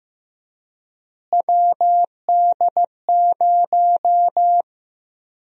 Morse code, heard as a keyed tone, about 15 wpm, 700 Hz. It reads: WD0
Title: USA Call Sign Prefixes - Repeated 2x - 15wpm